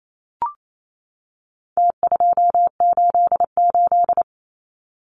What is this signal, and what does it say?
Morse code, heard as a keyed tone, about 28 wpm, 700 Hz. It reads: T288
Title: Numbers - Contest Style - 4-Digit 28wpm